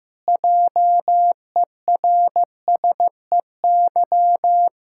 Morse code, heard as a keyed tone, about 15 wpm, 700 Hz. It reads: JERSEY
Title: Practice QSO - Exchange 1 - Beginning - 15wpm